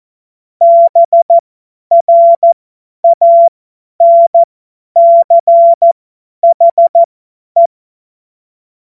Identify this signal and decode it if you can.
Morse code, heard as a keyed tone, about 14 wpm, spaced as for 10 wpm, 680 Hz. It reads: BRANCHE
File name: Morsekurs / Witze / Witz 08. August 2023 50 F70